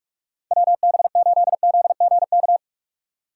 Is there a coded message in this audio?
Yes: Morse code from 0.5 to 2.6 s, heard as a keyed tone, about 47 wpm, 700 Hz. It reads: WB8ZGK